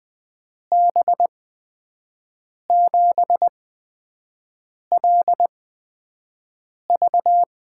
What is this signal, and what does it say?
Morse code, heard as a keyed tone, about 20 wpm, spaced as for 5 wpm, 700 Hz. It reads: B7LV
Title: USA Call Signs - 5 Characters - 8x Character Spacing - 20wpm